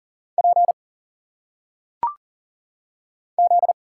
Morse code, heard as a keyed tone, about 39 wpm, 700 Hz. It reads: P Z
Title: Single Letters - Phonetic Pronunciation 40wpm